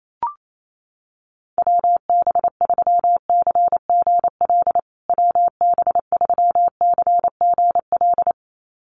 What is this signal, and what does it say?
Morse code, heard as a keyed tone, about 28 wpm, 700 Hz. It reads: WB3CGL WB3CGL
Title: USA Call Signs Repeated 2x 28wpm